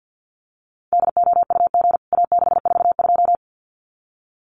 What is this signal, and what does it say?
Morse code, heard as a keyed tone, about 49 wpm, 700 Hz. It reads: DOUG A642
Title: CWT Exchange 50wpm